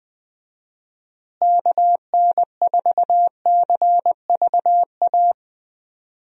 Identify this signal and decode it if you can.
Morse code, heard as a keyed tone, about 20 wpm, 700 Hz. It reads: KN4CVA
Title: USA Call Signs - 6 Characters - Repeated 2x - 20wpm